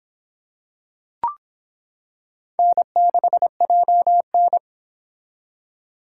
Morse code, heard as a keyed tone, about 25 wpm, 700 Hz. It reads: N6JN